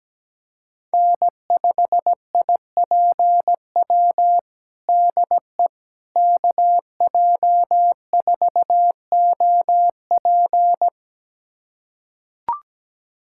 Morse code, heard as a keyed tone, about 17 wpm, 700 Hz. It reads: N5IPW DE KJ4OP
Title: USA Callsign DE Callsign - 5 Characters - 17wpm